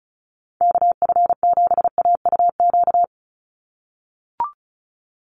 Morse code, heard as a keyed tone, about 34 wpm, 700 Hz. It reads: KF7AUQ
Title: USA Call Signs 35wpm